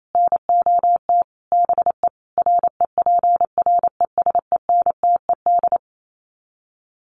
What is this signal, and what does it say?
Morse code, heard as a keyed tone, about 28 wpm, 700 Hz. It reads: NOT BE REPRESENTED
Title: Sets of 5 Words 28wpm